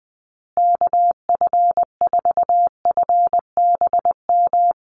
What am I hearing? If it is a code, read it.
Morse code, heard as a keyed tone, about 20 wpm, 700 Hz. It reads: KF4FBM
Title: USA Call Signs Repeated 2x 20wpm